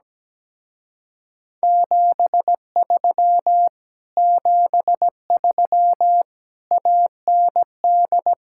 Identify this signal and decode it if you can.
Morse code, heard as a keyed tone, about 17 wpm, 700 Hz. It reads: 73 73 AND
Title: Speed-Racing Common QSO Phrases - 17wpm